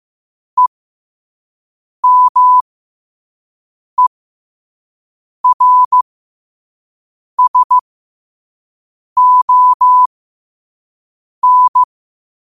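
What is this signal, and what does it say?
Morse code, heard as a keyed tone, about 15 wpm, spaced as for 5 wpm, 1000 Hz. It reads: EMERSON
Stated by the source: Morse code quote of the day at 5 wpm.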